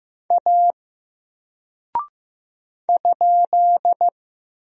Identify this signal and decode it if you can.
Morse code, heard as a keyed tone, about 15 wpm, 700 Hz. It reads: A ?
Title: Single Character (Letter:Number:Punctuation:Prosign) 15wpm